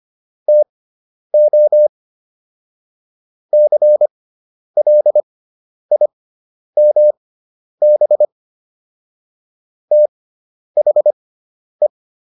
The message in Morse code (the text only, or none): TO CLIMB THE